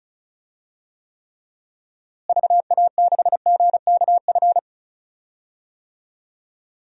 Morse code, heard as a keyed tone, about 34 wpm, 700 Hz. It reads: VA6GKF
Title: DXCC-100 Call Signs - 6 Characters - Repeated 3x - 35wpm